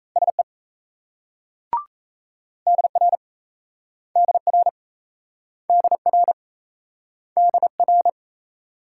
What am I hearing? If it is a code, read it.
Morse code, heard as a keyed tone, about 37 wpm, 700 Hz. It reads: SE DR DR DR DR